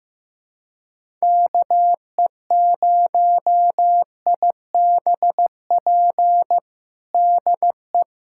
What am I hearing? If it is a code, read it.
Morse code, heard as a keyed tone, about 16 wpm, 700 Hz. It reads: KE0IBP DE
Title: USA Callsign DE Callsign - 6 Characters - 15wpm